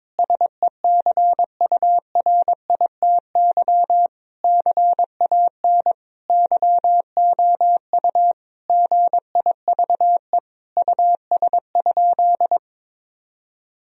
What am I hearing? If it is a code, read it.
Morse code, heard as a keyed tone, about 21 wpm, 700 Hz. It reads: SECURITY CAN YOU GIVE US?
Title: Sentences from Top 1000 Words - Review Part 4 - 22wpm